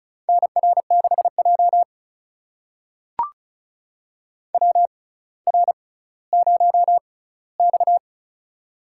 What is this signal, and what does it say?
Morse code, heard as a keyed tone, about 36 wpm, spaced as for 18 wpm, 700 Hz. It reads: NR6J WR0X
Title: USA Call Signs - 4 Characters - 6x Character Spacing - 35wpm